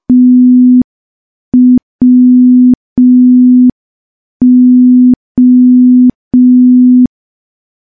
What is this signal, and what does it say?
Morse code, heard as a keyed tone, about 5 wpm, 250 Hz. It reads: TWO
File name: morse / morse-training-05WPM